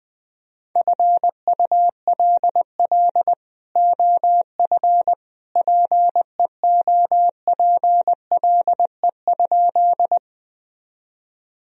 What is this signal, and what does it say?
Morse code, heard as a keyed tone, about 21 wpm, 700 Hz. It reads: FULL OF PEOPLE?
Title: Sentences from Top 1000 Words - Review Part 3 - 20wpm